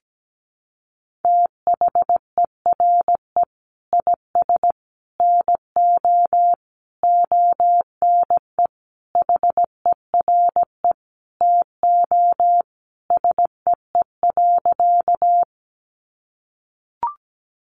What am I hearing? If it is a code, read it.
Morse code, heard as a keyed tone, about 17 wpm, 700 Hz. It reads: THERE IS NO ONE HERE TO SEE.
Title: Sentences from Top 200 Words 17wpm